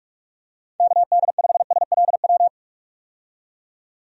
Morse code, heard as a keyed tone, about 46 wpm, 700 Hz. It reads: KD5SLW
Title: USA Call Signs - 6 Characters - 45wpm